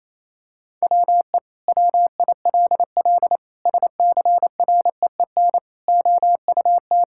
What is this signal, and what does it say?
Morse code, heard as a keyed tone, about 28 wpm, 700 Hz. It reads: WE WILL SCREEN OUT
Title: Sentences from Top 1000 Words - Review Part 6 - 28wpm